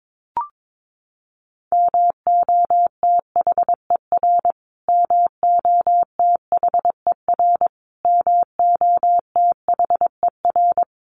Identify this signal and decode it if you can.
Morse code, heard as a keyed tone, about 22 wpm, 700 Hz. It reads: MOTHER MOTHER MOTHER
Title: Top 300 Words Repeated 22wpm